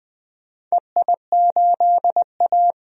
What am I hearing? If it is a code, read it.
Morse code, heard as a keyed tone, about 20 wpm, 700 Hz. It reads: EI8A
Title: DXCC-100 Call Signs - 4 Characters - 20wpm